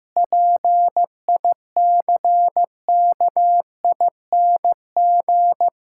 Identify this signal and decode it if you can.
Morse code, heard as a keyed tone, about 15 wpm, 700 Hz. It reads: PICKING
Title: Pangram Challenge v2 - 15wpm